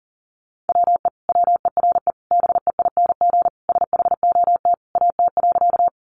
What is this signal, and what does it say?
Morse code, heard as a keyed tone, about 37 wpm, 700 Hz. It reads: WE WERE BEING SHOT AT.
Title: Sentences from Top 1100 Words 40wpm